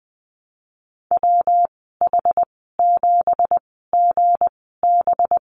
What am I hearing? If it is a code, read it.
Morse code, heard as a keyed tone, about 20 wpm, spaced as for 15 wpm, 700 Hz. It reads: WH7GB
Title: 186 Lesson 30g - Letter b - Call Signs 20wpm